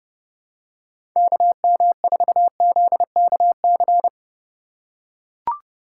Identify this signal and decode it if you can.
Morse code, heard as a keyed tone, about 30 wpm, 700 Hz. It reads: KM4ZKC
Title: USA Call Signs - 6 Characters - 2x Character Spacing - 30wpm